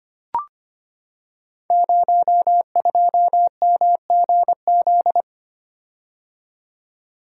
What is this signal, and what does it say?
Morse code, heard as a keyed tone, about 25 wpm, 690 Hz. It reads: 02MGZ
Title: Code Groups - 5 Letter:Numbers - 25wpm